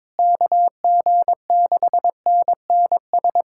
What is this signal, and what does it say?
Morse code, heard as a keyed tone, about 22 wpm, 700 Hz. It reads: KG6NNS
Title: USA Callsign DE Callsign 22wpm